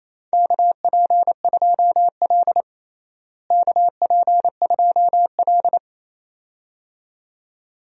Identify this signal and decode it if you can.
Morse code, heard as a keyed tone, about 28 wpm, 700 Hz. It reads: KP2L KP2L
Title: DXCC-100 Call Signs - 4 Characters - Repeated 2x - 28wpm